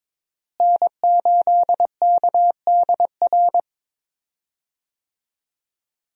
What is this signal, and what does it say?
Morse code, heard as a keyed tone, about 21 wpm, 700 Hz. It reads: N8KDR
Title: USA Call Signs - 5 Characters - Repeated 3x - 22wpm